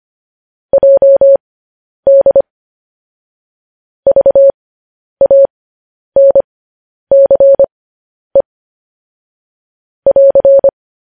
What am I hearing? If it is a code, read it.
Morse code, heard as a keyed tone, about 25 wpm, spaced as for 10 wpm, 550 Hz. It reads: JD VANCE <AR>
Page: news headlines in morse code at 10 wpm